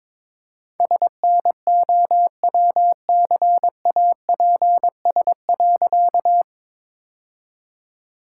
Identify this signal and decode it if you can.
Morse code, heard as a keyed tone, about 22 wpm, 700 Hz. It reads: SNOWCAPS.
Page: morse-code-ninja-practice-22wpm